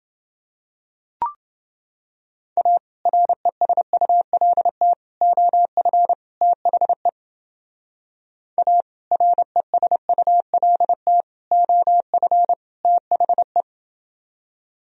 Morse code, heard as a keyed tone, about 29 wpm, 700 Hz. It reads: A RESULT OF THE A RESULT OF THE